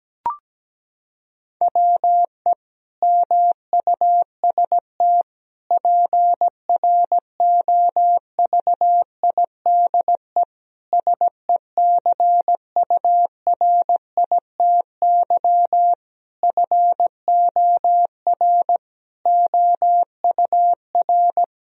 Morse code, heard as a keyed tone, about 17 wpm, 700 Hz. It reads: WE MUST PROVIDE SECURITY FOR OUR
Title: Sentences from Top 1000 Words - Review Part 4 - 17wpm